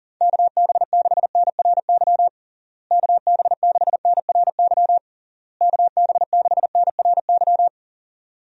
Morse code, heard as a keyed tone, about 40 wpm, 700 Hz. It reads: KB6NRY KB6NRY KB6NRY
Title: USA Call Signs - 6 Characters - Repeated 3x - 40wpm